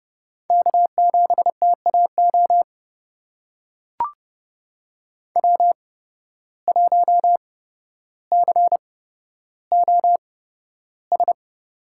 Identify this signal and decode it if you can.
Morse code, heard as a keyed tone, about 31 wpm, spaced as for 13 wpm, 700 Hz. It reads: K7TAO W1COS
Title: USA Call Signs - 5 Characters - 8x Character Spacing - 30wpm